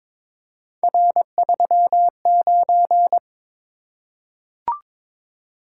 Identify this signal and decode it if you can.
Morse code, heard as a keyed tone, about 22 wpm, 700 Hz. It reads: R39
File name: practice / 22wpm / Three Letter-Number Combinations - 22wpm